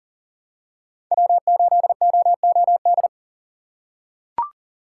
Morse code, heard as a keyed tone, about 40 wpm, 700 Hz. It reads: W8OOD
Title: USA Call Signs - 5 Characters - 3x Character Spacing - 40wpm